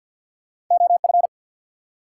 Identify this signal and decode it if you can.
Morse code, heard as a keyed tone, about 50 wpm, 700 Hz. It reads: OF
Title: Two Letter Words - 50wpm